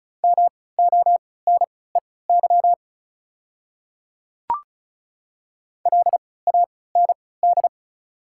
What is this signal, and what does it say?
Morse code, heard as a keyed tone, about 36 wpm, spaced as for 20 wpm, 690 Hz. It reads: MONEY LAND
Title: 3x Character Spacing - Top 500 Words - 35wpm